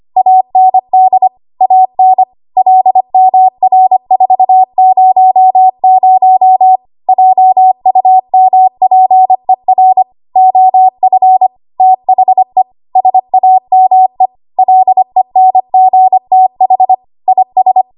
Morse code, heard as a keyed tone, about 25 wpm, 750 Hz. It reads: AND AN LMR400 JUMPER OF THE SAME LENGTH IS
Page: W1AW Web Code Practice Files